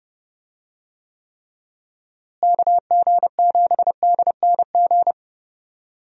Morse code, heard as a keyed tone, about 31 wpm, 700 Hz. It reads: KG7DNG
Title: USA Call Signs - 6 Characters - Repeated 2x - 30wpm